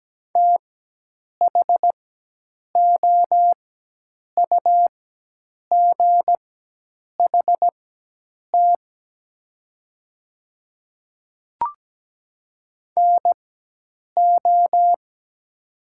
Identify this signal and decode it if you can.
Morse code, heard as a keyed tone, about 17 wpm, spaced as for 8 wpm, 700 Hz. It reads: THOUGHT NO
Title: 4x Character Spacing - Top 300 Words - 17wpm